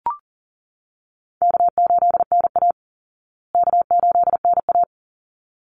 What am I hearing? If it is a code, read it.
Morse code, heard as a keyed tone, about 40 wpm, 700 Hz. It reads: K8NA K8NA
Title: Easy Call Signs Repeated 2x - 40wpm